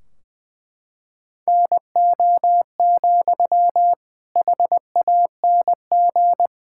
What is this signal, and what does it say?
Morse code, heard as a keyed tone, about 20 wpm, 700 Hz. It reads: NO, HANG
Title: Sentences from Top 1600 Words 20wpm